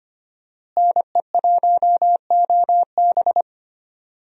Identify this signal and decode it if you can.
Morse code, heard as a keyed tone, about 25 wpm, 700 Hz. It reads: NE1OB
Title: USA Call Signs - 5 Characters - Repeated 2x - 25wpm